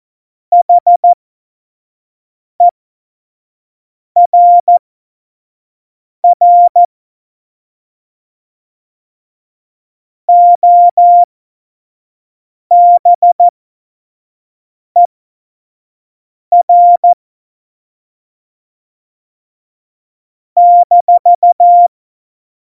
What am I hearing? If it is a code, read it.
Morse code, heard as a keyed tone, about 14 wpm, spaced as for 5 wpm, 700 Hz. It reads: HERR OBER -
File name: Morsekurs / Witze / Witz 17.Dezember 2019, 25 F70